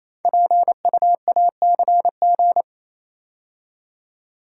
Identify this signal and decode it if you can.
Morse code, heard as a keyed tone, about 27 wpm, 700 Hz. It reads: PUACG
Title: Code Groups - 5 Letters - 28wpm